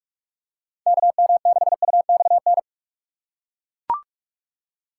Morse code, heard as a keyed tone, about 41 wpm, 700 Hz. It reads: KM6UXN
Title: USA Call Signs - 6 Characters - 2x Character Spacing - 45wpm